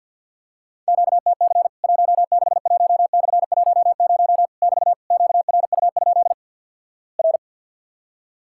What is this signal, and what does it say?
Morse code, heard as a keyed tone, about 50 wpm, 700 Hz. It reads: QTC 161/10 <BT> QRU? R
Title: Worked All Europe DX Contest - QTC Exchange with Repeats - 50wpm